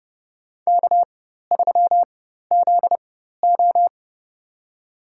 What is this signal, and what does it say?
Morse code, heard as a keyed tone, about 30 wpm, spaced as for 14 wpm, 700 Hz. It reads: K3ZO
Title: USA Call Signs - 4 Characters - 4x Character Spacing - 30wpm